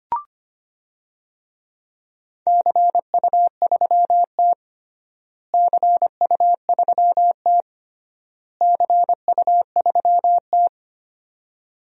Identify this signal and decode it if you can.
Morse code, heard as a keyed tone, about 25 wpm, 700 Hz. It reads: CU3T CU3T CU3T
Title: DXCC-100 Call Signs - 4 Characters - Repeated 3x - 25wpm